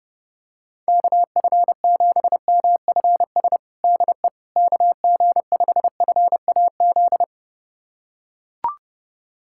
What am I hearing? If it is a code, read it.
Morse code, heard as a keyed tone, about 32 wpm, 700 Hz. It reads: KF7MFS DE KG5FAZ
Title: USA Callsign DE Callsign - 6 Characters - 30wpm